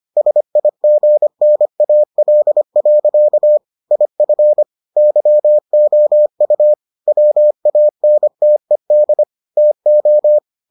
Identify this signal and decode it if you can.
Morse code, heard as a keyed tone, about 25 wpm, 600 Hz. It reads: SIGNAL. IF YOU WANTED TO